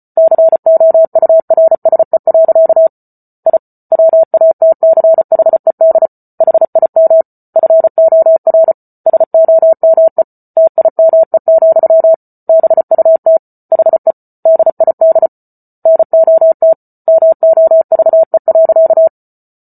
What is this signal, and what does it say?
Morse code, heard as a keyed tone, about 34 wpm, 650 Hz. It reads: COURSE. I WATCHED HIM FOR SOME TIME, BUT HE DID NOT MOVE.